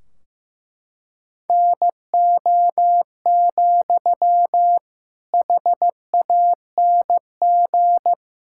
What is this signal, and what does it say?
Morse code, heard as a keyed tone, about 15 wpm, 700 Hz. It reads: NO, HANG
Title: Sentences from Top 1600 Words 15wpm